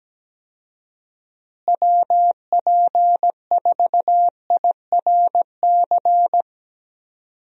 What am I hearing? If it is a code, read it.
Morse code, heard as a keyed tone, about 17 wpm, 700 Hz. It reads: WP4IRC
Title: USA Call Signs - 6 Characters - Repeated 3x - 17wpm